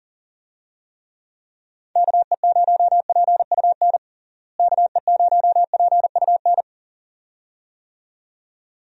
Morse code, heard as a keyed tone, about 40 wpm, 700 Hz. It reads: KE0PUN KE0PUN
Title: USA Call Signs - 6 Characters - Repeated 2x - 40wpm